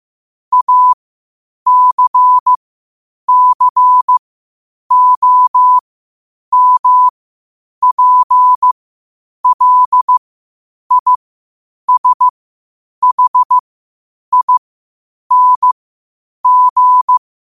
Morse code, heard as a keyed tone, about 15 wpm, spaced as for 8 wpm, 1000 Hz. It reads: ACCOMPLISHING
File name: qotdfiles / 8wpm